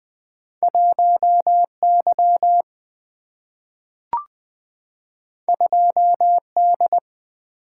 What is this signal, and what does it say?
Morse code, heard as a keyed tone, about 20 wpm, 700 Hz. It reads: 1Y 2D
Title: Single-Letter Single-Number 20wpm